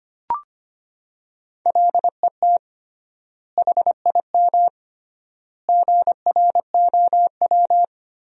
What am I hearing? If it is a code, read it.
Morse code, heard as a keyed tone, about 25 wpm, 700 Hz. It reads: LET HIM GROW
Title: 3x Word-Spacing - Sentences from Top 300 Words - 25wpm